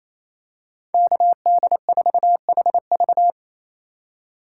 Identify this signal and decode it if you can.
Morse code, heard as a keyed tone, about 28 wpm, 700 Hz. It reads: KD4HV
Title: USA Call Signs - 5 Characters - 2x Character Spacing - 28wpm